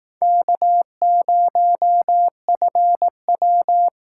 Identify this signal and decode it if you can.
Morse code, heard as a keyed tone, about 19 wpm, 700 Hz. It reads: K0FW
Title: Speed Racing - USA Call Signs 15wpm